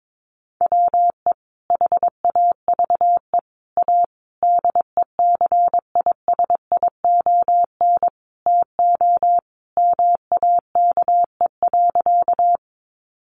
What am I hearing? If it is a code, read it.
Morse code, heard as a keyed tone, about 22 wpm, 700 Hz. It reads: WE HAVE A DECISION TO MAKE.
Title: Sentences from Top 1000 Words 22wpm